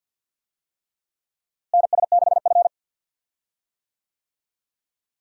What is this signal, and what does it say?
Morse code, heard as a keyed tone, about 53 wpm, 700 Hz. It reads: NS6F
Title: USA Call Signs - 4 Characters - Repeated 3x - 50wpm